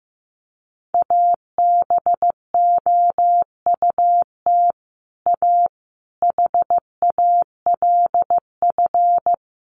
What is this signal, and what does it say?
Morse code, heard as a keyed tone, about 14 wpm, 700 Hz. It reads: ABOUT A HALF
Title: Sentences from Top 1400 Words 15wpm